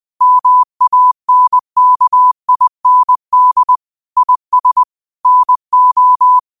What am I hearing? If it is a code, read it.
Morse code, heard as a keyed tone, about 20 wpm, 1000 Hz. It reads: MANKIND IS NO
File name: qotdfiles / 20wpm